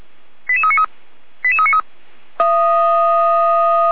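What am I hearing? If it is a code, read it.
Five-tone selective call (ZVEI), heard as quick runs of five tones, tones of about 70 ms. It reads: 88393 88393
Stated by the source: ZVEI five-tone call received on SDR.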